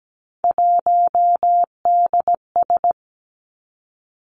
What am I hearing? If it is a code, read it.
Morse code, heard as a keyed tone, about 17 wpm, 700 Hz. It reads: 1DS
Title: Three Challenging Letter-Number Combinations 17wpm